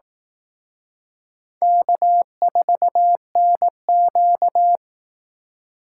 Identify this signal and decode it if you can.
Morse code, heard as a keyed tone, about 18 wpm, 700 Hz. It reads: K4NQ
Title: Speed Racing - USA 1x2 Call Signs 15wpm